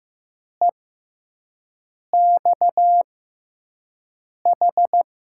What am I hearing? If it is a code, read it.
Morse code, heard as a keyed tone, about 15 wpm, spaced as for 5 wpm, 700 Hz. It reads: EXH
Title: USA Call Signs - 5 Characters - 6x Character Spacing - 15wpm